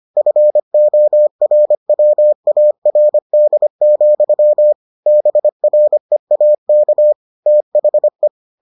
Morse code, heard as a keyed tone, about 25 wpm, 600 Hz. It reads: FORWARD, BREAK THE